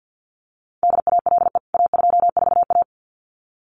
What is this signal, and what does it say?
Morse code, heard as a keyed tone, about 49 wpm, 700 Hz. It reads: DALE A24A
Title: CWT Exchange 50wpm